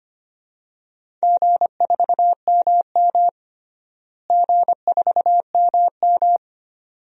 Morse code, heard as a keyed tone, about 25 wpm, 700 Hz. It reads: G4MM G4MM
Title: DXCC-100 Call Signs - 4 Characters - Repeated 2x - 25wpm